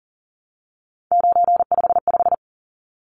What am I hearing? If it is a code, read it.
Morse code, heard as a keyed tone, about 40 wpm, 700 Hz. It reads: 955
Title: Numbers - 3-Digit 40wpm